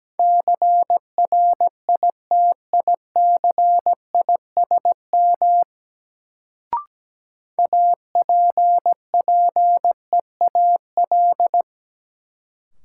Morse code, heard as a keyed tone, about 17 wpm, 700 Hz. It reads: CRITICISM APPEAL
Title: Sentences from Top 1300 Words - New Word Drill - 17wpm